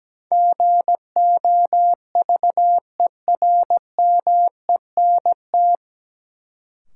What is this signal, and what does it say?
Morse code, heard as a keyed tone, about 17 wpm, 700 Hz. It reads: GOVERMENT
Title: Sentences from Top 1300 Words - New Word Drill - 17wpm